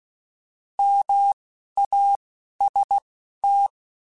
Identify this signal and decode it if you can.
Morse code, heard as a keyed tone, about 16 wpm, 800 Hz. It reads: M A S T
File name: Alphabetical / Alphabetical mp3s / Ms @10